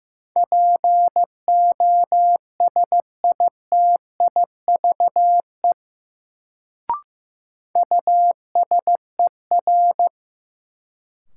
Morse code, heard as a keyed tone, about 15 wpm, 700 Hz. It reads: POSITIVE USER
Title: Sentences from Top 1000 Words - Review Part 5 - New Word Drill - 15wpm